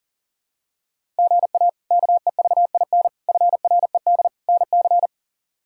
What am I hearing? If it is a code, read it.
Morse code, heard as a keyed tone, about 40 wpm, 700 Hz. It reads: GA KEVIN FRED NC
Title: K1USN SST QSO Practice - 40wpm